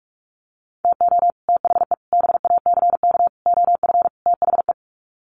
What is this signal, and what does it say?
Morse code, heard as a keyed tone, about 44 wpm, 700 Hz. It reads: TO THE BACK OF THE
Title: Sets of 5 Words 45wpm